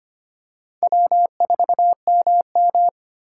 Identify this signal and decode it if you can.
Morse code, heard as a keyed tone, about 25 wpm, 700 Hz. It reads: W4MM
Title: USA Call Signs - 2x Character Spacing - 25wpm